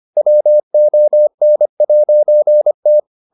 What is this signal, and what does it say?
Morse code, heard as a keyed tone, about 25 wpm, 600 Hz. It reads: WON'T